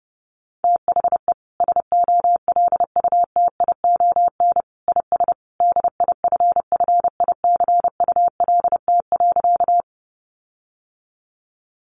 Morse code, heard as a keyed tone, about 30 wpm, 700 Hz. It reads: THE SOLUTION IS DIFFICULT.
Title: Sentences from Top 1300 Words 30wpm